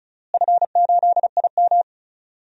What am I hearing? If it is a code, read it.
Morse code, heard as a keyed tone, about 36 wpm, 700 Hz. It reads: F8IM
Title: DXCC-100 Call Signs - 4 Characters - Repeated 3x - 35wpm